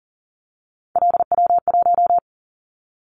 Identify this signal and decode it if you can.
Morse code, heard as a keyed tone, about 40 wpm, 700 Hz. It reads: LW1
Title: Callsign Prefixes for Top 100 DXCC - 40wpm